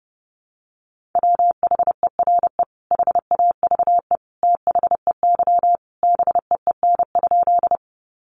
Morse code, heard as a keyed tone, about 30 wpm, 700 Hz. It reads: WHERE HAVE THEY BEEN?
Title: Sentences from Top 200 Words 30wpm